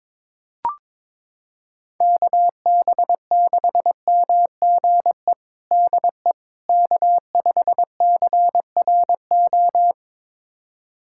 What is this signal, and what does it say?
Morse code, heard as a keyed tone, about 23 wpm, 700 Hz. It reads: KB6MGE DE K5CRO
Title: USA Callsign DE Callsign 22wpm